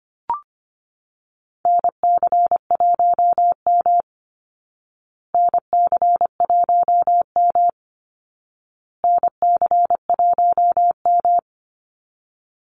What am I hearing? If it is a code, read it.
Morse code, heard as a keyed tone, about 25 wpm, 700 Hz. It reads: NC1M NC1M NC1M
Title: Call Signs Repeated 2x - 25wpm